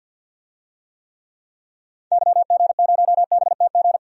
Morse code, heard as a keyed tone, about 48 wpm, 700 Hz. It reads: YG0BTG
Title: DXCC-100 Call Signs - 6 Characters - 50wpm